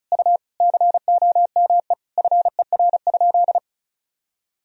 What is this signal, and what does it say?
Morse code, heard as a keyed tone, about 34 wpm, 700 Hz. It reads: U COME FER?
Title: Sentences from Top 100 Words and Abbreviations 35wpm